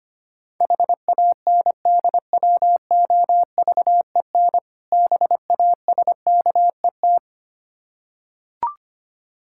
Morse code, heard as a keyed tone, about 25 wpm, 690 Hz. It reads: HANDWOVEN BASKET
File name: practice / 25wpm / Difficult Words with Hints - 25wpm